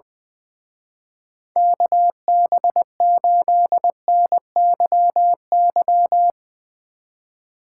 Morse code, heard as a keyed tone, about 22 wpm, 700 Hz. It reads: KB8NYY
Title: Speed Racing - USA 2x3 Call Signs 20wpm